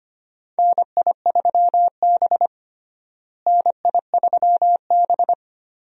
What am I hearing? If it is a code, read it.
Morse code, heard as a keyed tone, about 25 wpm, 700 Hz. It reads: NI3B NI3B
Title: USA Call Signs - 4 Characters - Repeated 2x - 25wpm